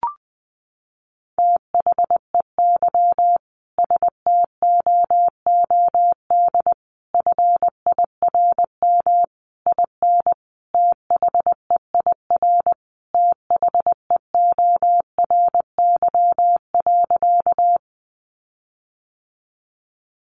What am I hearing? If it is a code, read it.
Morse code, heard as a keyed tone, about 20 wpm, 700 Hz. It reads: THEY STOOD FIRM IN THEIR THEORY.
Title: Sentences from Top 1400 Words 20wpm